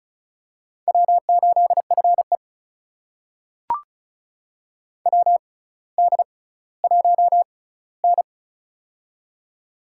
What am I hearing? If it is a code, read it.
Morse code, heard as a keyed tone, about 36 wpm, spaced as for 18 wpm, 700 Hz. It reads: W8FE WD1N
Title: USA Call Signs - 4 Characters - 6x Character Spacing - 35wpm